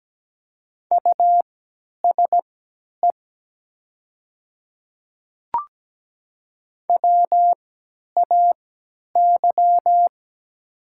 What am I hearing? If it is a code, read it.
Morse code, heard as a keyed tone, about 17 wpm, spaced as for 10 wpm, 700 Hz. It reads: USE WAY
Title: 3x Character Spacing - Top 100 Words - 17wpm